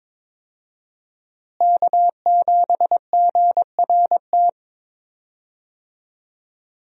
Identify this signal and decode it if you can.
Morse code, heard as a keyed tone, about 21 wpm, 690 Hz. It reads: K7GRT
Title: USA Call Signs - 5 Characters - Repeated 3x - 22wpm